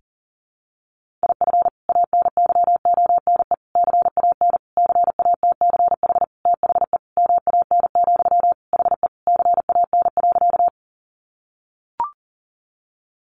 Morse code, heard as a keyed tone, about 40 wpm, 700 Hz. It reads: IF ANYONE CAN CATCH THE MAN, HE CAN.
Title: Sentences from Top 1500 Words 40wpm